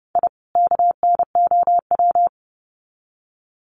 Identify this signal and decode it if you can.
Morse code, heard as a keyed tone, about 30 wpm, 700 Hz. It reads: I KNOW
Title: Sets of 2 Words 30wpm